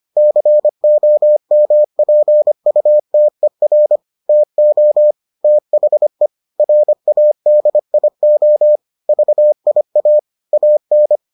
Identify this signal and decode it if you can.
Morse code, heard as a keyed tone, about 25 wpm, 600 Hz. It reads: COMPUTER TO THE RADIO VIA AN